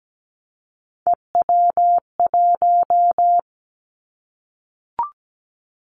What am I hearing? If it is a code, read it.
Morse code, heard as a keyed tone, about 17 wpm, 700 Hz. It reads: EW1
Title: Callsign Prefixes for Top 50 DXCC - 17wpm